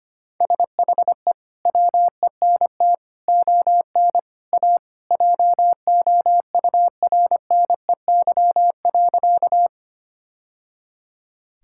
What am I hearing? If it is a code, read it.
Morse code, heard as a keyed tone, about 25 wpm, 700 Hz. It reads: SHE WENT ON A JOURNEY.
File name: practice / 25wpm / Sentences with 2-Word Phrases - 25wpm